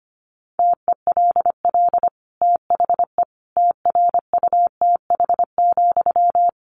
Morse code, heard as a keyed tone, about 25 wpm, 700 Hz. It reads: TELL THE TRUTH,
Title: Sentences from Top 800 Words 25wpm